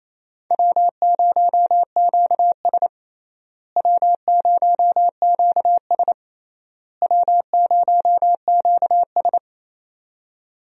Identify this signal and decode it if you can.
Morse code, heard as a keyed tone, about 28 wpm, 700 Hz. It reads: W0QS W0QS W0QS
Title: USA Call Signs - 4 Characters - Repeated 3x - 28wpm